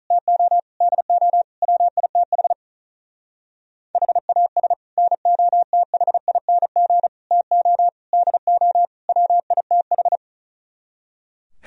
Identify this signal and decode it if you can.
Morse code, heard as a keyed tone, about 37 wpm, 700 Hz. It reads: TO DO WITH HAS NOTHING TO DO WITH
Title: Speed-Racing - Sets of 5 Words 35wpm